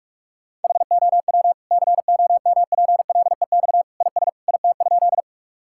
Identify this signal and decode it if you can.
Morse code, heard as a keyed tone, about 46 wpm, 700 Hz. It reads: HOW COMPLEX IS IT?
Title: Sentences from Top 1000 Words - Review Part 6 - 45wpm